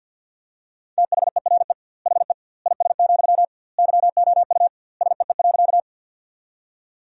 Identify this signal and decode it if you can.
Morse code, heard as a keyed tone, about 50 wpm, 700 Hz. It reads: THERE HE IS, YOU SEE.
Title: 2x Word-Spacing - Sentences from Top 100 Words - 50wpm